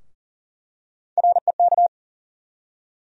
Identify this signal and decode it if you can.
Morse code, heard as a keyed tone, about 40 wpm, 700 Hz. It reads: REK
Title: USA Call Sign Postfixes - 3 Letters 40wpm